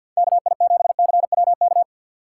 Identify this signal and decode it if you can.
Morse code, heard as a keyed tone, about 64 wpm, 700 Hz. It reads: KI7CWK